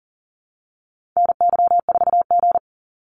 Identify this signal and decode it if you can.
Morse code, heard as a keyed tone, about 40 wpm, 700 Hz. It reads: NY4G
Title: NA SOTA Activator Callsigns 40wpm